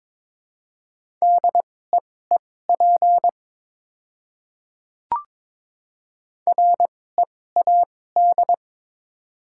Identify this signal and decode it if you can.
Morse code, heard as a keyed tone, about 21 wpm, spaced as for 16 wpm, 700 Hz. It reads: DEEP READ
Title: 2x Character Spacing - Top 500 Words - 22wpm